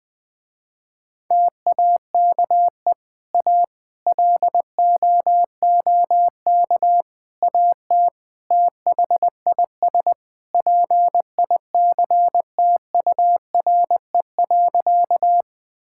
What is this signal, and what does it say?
Morse code, heard as a keyed tone, about 20 wpm, 700 Hz. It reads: TAKE A LOOK AT THIS PICTURE.
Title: Sentences with 5-Word Phrases - Spoken Beginning - 20wpm